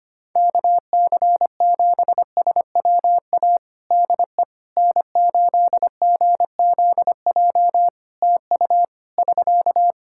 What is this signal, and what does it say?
Morse code, heard as a keyed tone, about 25 wpm, 700 Hz. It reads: KC7SWA DE N8GZJ TU <SK>
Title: Practice QSO - Exchange 4 - Ending - 25wpm